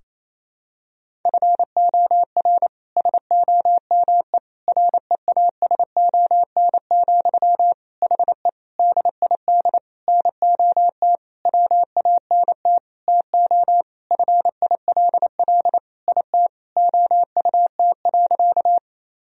Morse code, heard as a keyed tone, about 28 wpm, 700 Hz. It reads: FOR SOME REASON, HE DID NOT WANT TO FILL IT OUT.
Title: Sentences from Top 1000 Words - Review Part 3 - 28wpm